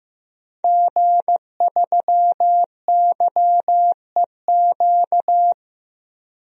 Morse code, heard as a keyed tone, about 15 wpm, 700 Hz. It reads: G3YEQ
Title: DXCC-100 Call Signs - 5 Characters - 15wpm